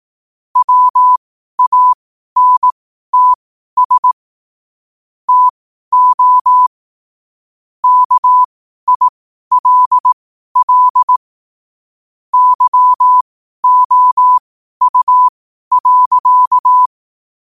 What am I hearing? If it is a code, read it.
Morse code, heard as a keyed tone, about 18 wpm, spaced as for 12 wpm, 1000 Hz. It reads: WANTS TO KILL YOU.
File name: qotdfiles / 12wpm